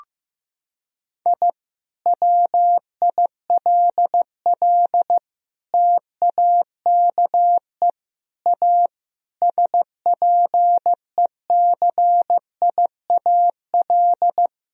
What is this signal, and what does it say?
Morse code, heard as a keyed tone, about 15 wpm, 700 Hz. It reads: I WILL TAKE A SPECIAL
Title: Sentences from Top 1000 Words - Review Part 3 - 15wpm